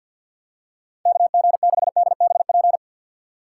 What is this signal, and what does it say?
Morse code, heard as a keyed tone, about 52 wpm, 700 Hz. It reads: KG6DBP
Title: USA Call Signs - 6x Character Spacing - 50wpm